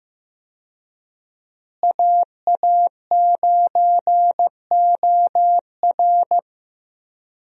Morse code, heard as a keyed tone, about 15 wpm, 700 Hz. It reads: AA9OR
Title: USA Call Signs - 5 Characters - Repeated 3x - 15wpm